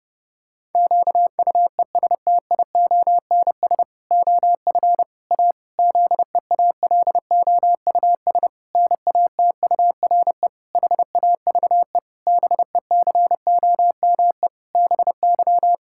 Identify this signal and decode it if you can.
Morse code, heard as a keyed tone, about 30 wpm, 700 Hz. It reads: QUESTIONS OF A ZEALOUS NATURE HAVE BECOME BY
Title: Pangram Challenge v2 - 30wpm